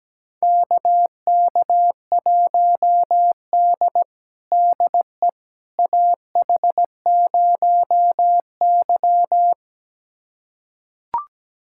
Morse code, heard as a keyed tone, about 17 wpm, 700 Hz. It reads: KK1D DE AH0Y
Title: USA Callsign DE Callsign - 4 Characters - 17wpm